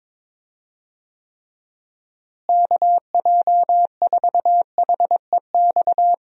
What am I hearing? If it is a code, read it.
Morse code, heard as a keyed tone, about 22 wpm, 700 Hz. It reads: KJ4HEX
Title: USA Call Signs - 6 Characters - Repeated 2x - 22wpm